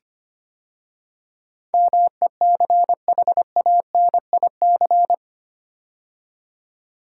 Morse code, heard as a keyed tone, about 25 wpm, 700 Hz. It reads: MECHANIC
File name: 8 Letter Words - First 3 Letters Given - 25wpm